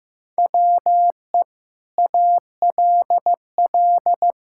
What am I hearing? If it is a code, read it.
Morse code, heard as a keyed tone, about 16 wpm, 700 Hz. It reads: WE ALL
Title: Sentences from Top 1000 Words - Review Part 5 - 15wpm